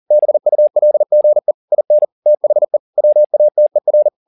Morse code, heard as a keyed tone, about 40 wpm, 600 Hz. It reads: BULGE IN THE WATER